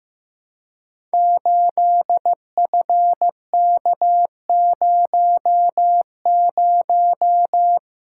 Morse code, heard as a keyed tone, about 15 wpm, 700 Hz. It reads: 8FK00
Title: Code Groups - 5 Letter:Numbers - 15wpm